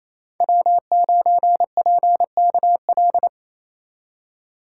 Morse code, heard as a keyed tone, about 28 wpm, 700 Hz. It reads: W9PKL
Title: USA Call Signs - 5 Characters - 2x Character Spacing - 28wpm